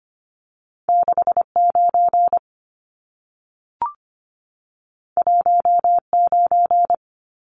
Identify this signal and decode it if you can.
Morse code, heard as a keyed tone, about 25 wpm, 700 Hz. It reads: 69 19
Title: Numbers - 2-Digit 25wpm